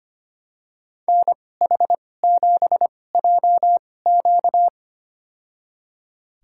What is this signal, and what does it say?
Morse code, heard as a keyed tone, about 25 wpm, spaced as for 18 wpm, 700 Hz. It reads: NH7JQ
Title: USA Call Signs - 5 Characters - 2x Character Spacing - 25wpm